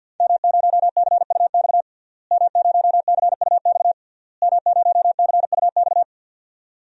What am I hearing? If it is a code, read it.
Morse code, heard as a keyed tone, about 52 wpm, 700 Hz. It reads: M0CUX M0CUX M0CUX
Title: DXCC-100 Call Signs - 5 Characters - Repeated 3x - 50wpm